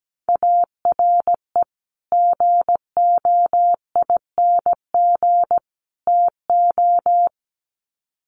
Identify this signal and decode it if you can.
Morse code, heard as a keyed tone, about 17 wpm, 700 Hz. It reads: ARE GOING TO
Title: Sets of 3 Words 17wpm